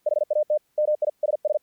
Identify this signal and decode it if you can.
Morse code, heard as a keyed tone, about 48 wpm, 600 Hz. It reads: HAT MISS
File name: archived / CRSnet